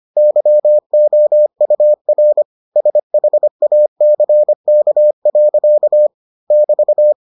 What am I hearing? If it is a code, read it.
Morse code, heard as a keyed tone, about 25 wpm, 600 Hz. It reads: YOUR SHACK. <BT>